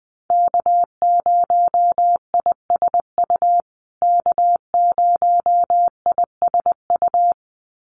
Morse code, heard as a keyed tone, about 20 wpm, 700 Hz. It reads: K0ISU K0ISU
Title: USA Call Signs Repeated 2x 20wpm